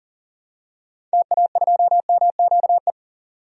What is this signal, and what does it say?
Morse code, heard as a keyed tone, about 39 wpm, 700 Hz. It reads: TA2MQE
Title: DXCC-100 Call Signs - 6 Characters - 40wpm